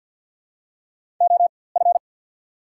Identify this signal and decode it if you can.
Morse code, heard as a keyed tone, about 53 wpm, spaced as for 24 wpm, 700 Hz. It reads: OF
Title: 4x Character Spacing - Top 500 Words - 50wpm